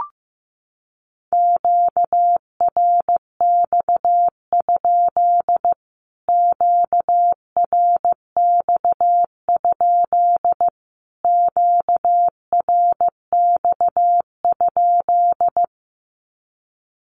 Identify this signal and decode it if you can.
Morse code, heard as a keyed tone, about 15 wpm, 700 Hz. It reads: QRX? QRX? QRX?
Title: Q-Codes Repeated 3x 15wpm